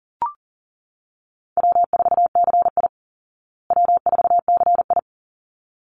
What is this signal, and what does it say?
Morse code, heard as a keyed tone, about 40 wpm, 700 Hz. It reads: W4CI W4CI
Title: Easy Call Signs Repeated 2x - 40wpm